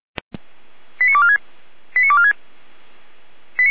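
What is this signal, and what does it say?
Five-tone selective call (ZVEI), heard as quick runs of five tones, tones of about 80 ms. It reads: 89246 89246
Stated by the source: Five-tone ZVEI selective call received by SDR.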